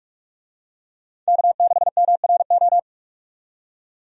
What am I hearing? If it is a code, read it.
Morse code, heard as a keyed tone, about 45 wpm, 700 Hz. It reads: K6MRO
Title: USA Call Signs - 5 Characters - 45wpm